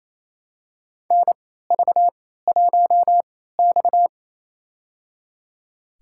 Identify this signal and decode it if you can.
Morse code, heard as a keyed tone, about 28 wpm, spaced as for 16 wpm, 700 Hz. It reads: NV1X
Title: USA Call Signs - 4 Characters - 3x Character Spacing - 28wpm